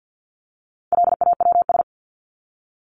Morse code, heard as a keyed tone, about 50 wpm, 700 Hz. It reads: LAWS
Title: ES:S Suffix - Top 100 Words with Base Words 50wpm